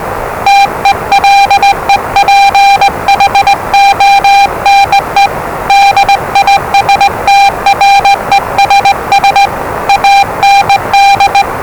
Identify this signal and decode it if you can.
Morse code, heard as a keyed tone, about 20 wpm, 800 Hz, with noise. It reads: TELEPHONE DISTRESS AND
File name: WellingtonRadioZlwCloseDownOnHfAndMf / 01-Zlw 4277khz 417.5 qrt until 0925z